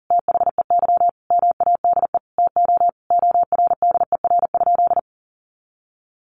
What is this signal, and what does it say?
Morse code, heard as a keyed tone, about 40 wpm, 700 Hz. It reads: THEY MADE TO ORDER?
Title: Sentences from Top 400 Words 40wpm